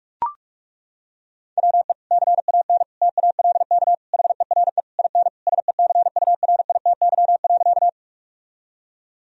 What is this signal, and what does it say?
Morse code, heard as a keyed tone, about 41 wpm, 700 Hz. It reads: WE CAN TALK HERE IN SECURITY.
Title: Sentences from Top 1000 Words - Review Part 4 - 45wpm